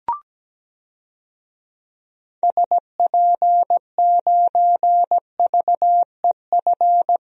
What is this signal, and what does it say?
Morse code, heard as a keyed tone, about 17 wpm, 700 Hz. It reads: SP9VEF
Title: DXCC-100 Call Signs - 6 Characters - Repeated 2x - 17wpm